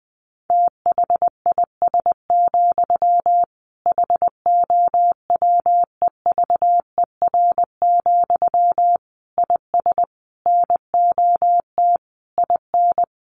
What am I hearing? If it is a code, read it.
Morse code, heard as a keyed tone, about 20 wpm, 700 Hz. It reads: THIS, HOWEVER, IS NOT IN
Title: Sentences from Top 1200 Words 20wpm